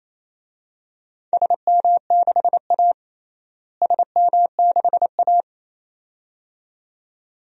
Morse code, heard as a keyed tone, about 28 wpm, 700 Hz. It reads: SM6A SM6A
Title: DXCC-100 Call Signs - 4 Characters - Repeated 2x - 28wpm